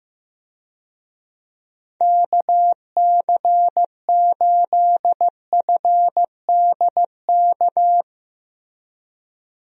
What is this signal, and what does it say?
Morse code, heard as a keyed tone, about 16 wpm, 700 Hz. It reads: KC8FDK
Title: USA Call Signs - 6 Characters - Repeated 3x - 15wpm